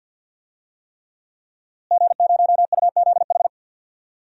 Morse code, heard as a keyed tone, about 47 wpm, 700 Hz. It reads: G0UZH
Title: DXCC-100 Call Signs - 5 Characters - Repeated 3x - 50wpm